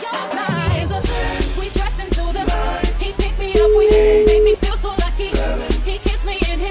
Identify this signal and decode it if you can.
Ringback tone (Europe, 425 Hz), with noise.